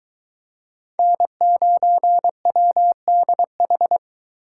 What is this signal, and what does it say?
Morse code, heard as a keyed tone, about 22 wpm, 700 Hz. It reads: N9WDH